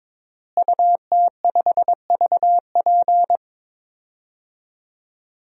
Morse code, heard as a keyed tone, about 22 wpm, 700 Hz. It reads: UT5VP
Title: DXCC-100 Call Signs - 5 Characters - Repeated 3x - 22wpm